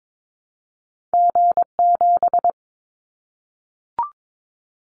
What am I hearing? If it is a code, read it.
Morse code, heard as a keyed tone, about 22 wpm, 700 Hz. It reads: G7
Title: Callsign Prefixes for Top 25 DXCC - Repeated 3x - 22wpm